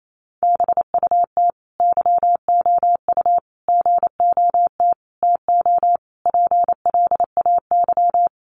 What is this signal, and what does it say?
Morse code, heard as a keyed tone, about 28 wpm, 700 Hz. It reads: BUT YOU GOT TO PLAY
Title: Sentences from Top 300 Words 28wpm